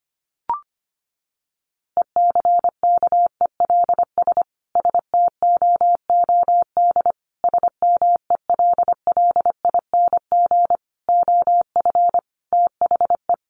Morse code, heard as a keyed tone, about 25 wpm, 700 Hz. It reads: ECKELS STOOD SMELLING OF THE
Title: A Sound Of Thunder 25wpm